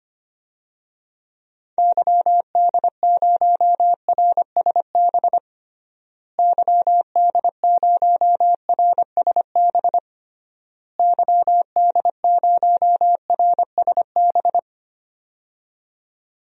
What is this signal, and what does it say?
Morse code, heard as a keyed tone, about 25 wpm, 700 Hz. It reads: YD0RSB YD0RSB YD0RSB
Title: DXCC-100 Call Signs - 6 Characters - Repeated 3x - 25wpm